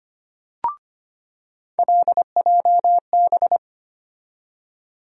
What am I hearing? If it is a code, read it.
Morse code, heard as a keyed tone, about 25 wpm, 700 Hz. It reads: LJB
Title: USA Call Sign Postfixes - 3 Letters 25wpm